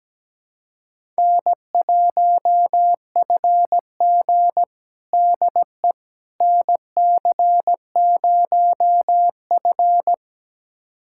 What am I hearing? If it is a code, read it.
Morse code, heard as a keyed tone, about 17 wpm, 700 Hz. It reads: N1FG DE NC0F
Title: USA Callsign DE Callsign - 4 Characters - 17wpm